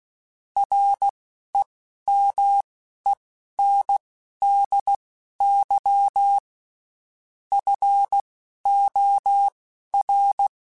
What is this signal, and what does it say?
Morse code, heard as a keyed tone, about 16 wpm, spaced as for 12 wpm, 800 Hz. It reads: REMENDY FOR
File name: Quotes Spaced / Quotes Spaced-mp3s / Temper sp @15